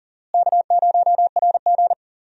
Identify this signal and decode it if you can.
Morse code, heard as a keyed tone, about 38 wpm, 700 Hz. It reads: K0RG
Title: USA Call Signs - 4 Characters - Repeated 2x - 40wpm